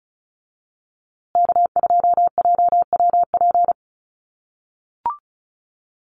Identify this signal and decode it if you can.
Morse code, heard as a keyed tone, about 34 wpm, 700 Hz. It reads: K2JWP
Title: USA Call Signs Repeated 3x 35wpm